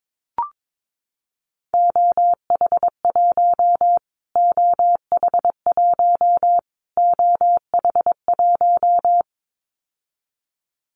Morse code, heard as a keyed tone, about 22 wpm, 700 Hz. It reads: OH1 OH1 OH1
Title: Callsign Prefixes for Top 25 DXCC - Repeated 3x - 22wpm